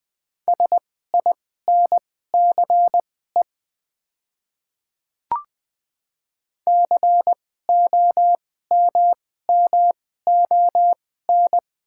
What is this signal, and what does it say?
Morse code, heard as a keyed tone, about 20 wpm, spaced as for 15 wpm, 700 Hz. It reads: SINCE COMMON
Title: 2x Character Spacing - Top 500 Words - 20wpm